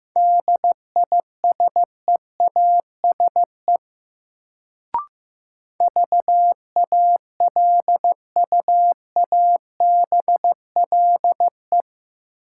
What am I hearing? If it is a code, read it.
Morse code, heard as a keyed tone, about 16 wpm, 700 Hz. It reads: DISEASE VALUABLE
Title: Sentences from Top 1100 Words - New Word Drill Alt - 15wpm